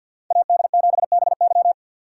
Morse code, heard as a keyed tone, about 52 wpm, 700 Hz. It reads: AD7BY
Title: USA Call Signs - 5 Characters - 6x Character Spacing - 50wpm